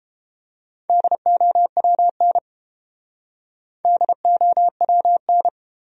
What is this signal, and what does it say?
Morse code, heard as a keyed tone, about 30 wpm, 700 Hz. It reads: DOWN DOWN